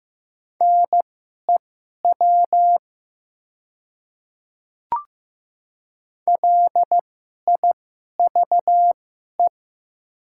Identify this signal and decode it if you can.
Morse code, heard as a keyed tone, about 15 wpm, spaced as for 11 wpm, 700 Hz. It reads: NEW LIVE
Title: 2x Character Spacing - Top 200 Words - 15wpm